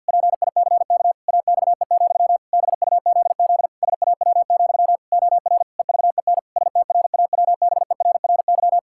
Morse code, heard as a keyed tone, about 50 wpm, 700 Hz. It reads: PICK AXE, BUZZ SAW, OR EVEN STRAWBERRY
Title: Pangram Challenge v2 - 50wpm